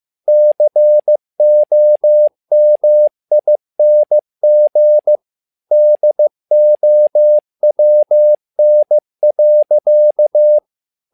Morse code, heard as a keyed tone, about 15 wpm, 600 Hz. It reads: COMING DOWN.
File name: learning center / tower trouble / tower trouble 15wpm